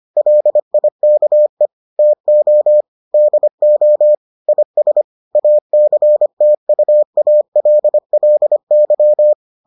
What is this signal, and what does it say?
Morse code, heard as a keyed tone, about 25 wpm, 600 Hz. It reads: LIKE TO DO IS ACTUALLY